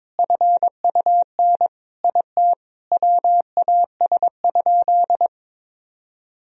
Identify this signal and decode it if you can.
Morse code, heard as a keyed tone, about 21 wpm, 700 Hz. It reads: FUN IT WAS?
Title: Sentences from Top 1000 Words - Review Part 4 - 22wpm